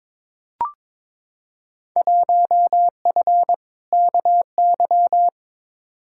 Morse code, heard as a keyed tone, about 22 wpm, 700 Hz. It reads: 1F KY
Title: Field Day Exchanges Encore - 22wpm